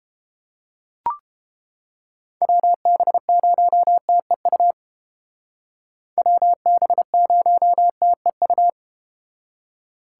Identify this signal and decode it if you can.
Morse code, heard as a keyed tone, about 32 wpm, 700 Hz. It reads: WB0TEU WB0TEU